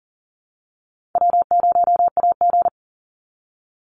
Morse code, heard as a keyed tone, about 40 wpm, 700 Hz. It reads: W0AG
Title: Easy Call Signs Repeated 3x - 40wpm